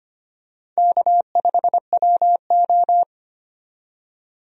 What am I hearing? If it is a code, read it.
Morse code, heard as a keyed tone, about 25 wpm, 700 Hz. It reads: K5WO
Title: USA Call Signs - 4 Characters - 25wpm